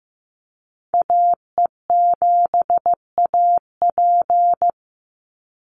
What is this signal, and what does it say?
Morse code, heard as a keyed tone, about 15 wpm, 700 Hz. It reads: AE7AP
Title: NA SOTA Chaser Callsigns 15wpm